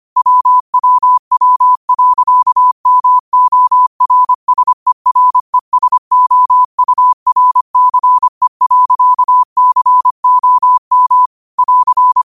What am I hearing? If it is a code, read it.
Morse code, heard as a keyed tone, about 25 wpm, 1000 Hz. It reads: WWW.MORSERESOURCE.COM <AR>
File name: qotdfiles / 25wpm